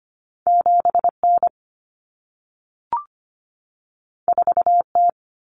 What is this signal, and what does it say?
Morse code, heard as a keyed tone, about 25 wpm, 700 Hz. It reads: 7N 4T
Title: Common Cut Numbers - 2-Digit 25wpm